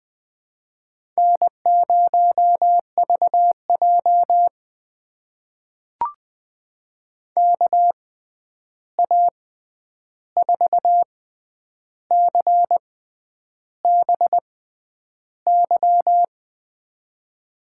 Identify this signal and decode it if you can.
Morse code, heard as a keyed tone, about 21 wpm, spaced as for 9 wpm, 700 Hz. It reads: N0VJ KA4CBY
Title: USA Call Signs - 6x Character Spacing - 20wpm